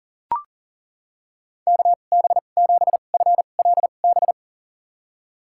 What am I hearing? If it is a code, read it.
Morse code, heard as a keyed tone, about 38 wpm, spaced as for 28 wpm, 700 Hz. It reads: KB7FLB
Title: USA Call Signs - 6 Characters - 2x Character Spacing - 40wpm